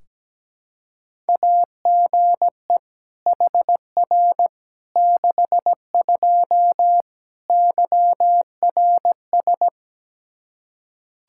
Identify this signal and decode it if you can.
Morse code, heard as a keyed tone, about 17 wpm, 700 Hz. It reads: AGE HR 62 YRS
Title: Practice QSO - Exchange 3 - More Info - 17wpm